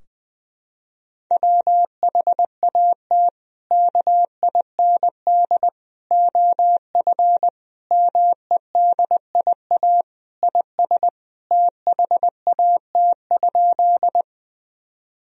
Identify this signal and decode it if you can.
Morse code, heard as a keyed tone, about 21 wpm, 700 Hz. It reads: WHAT KIND OF MEDIA IS THAT?
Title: Sentences from Top 1000 Words - Review Part 4 - 20wpm